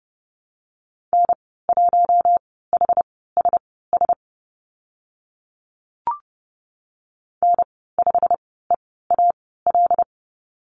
Morse code, heard as a keyed tone, about 30 wpm, spaced as for 17 wpm, 700 Hz. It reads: N1HSS N5EAL
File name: Lesson 15g - Number 5 - Call Signs 30wpm